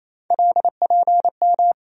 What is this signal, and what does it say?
Morse code, heard as a keyed tone, about 28 wpm, 700 Hz. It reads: LPM
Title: USA Call Sign Postfixes - 3 Letters 28wpm